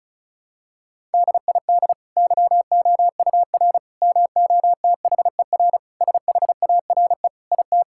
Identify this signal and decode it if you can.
Morse code, heard as a keyed tone, about 36 wpm, 700 Hz. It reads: DID YOUR MOTHER SHARE IT
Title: Sentences from Top 1000 Words - Review Part 6 - 35wpm